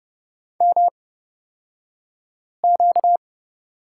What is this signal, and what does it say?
Morse code, heard as a keyed tone, about 29 wpm, spaced as for 5 wpm, 700 Hz. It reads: MQ
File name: Lesson 38d - Letter q - Two Characters 30wpm